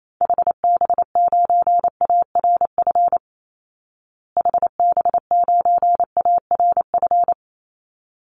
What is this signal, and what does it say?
Morse code, heard as a keyed tone, about 28 wpm, 700 Hz. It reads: HB9ARF HB9ARF
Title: Call Signs Repeated 3x - 28wpm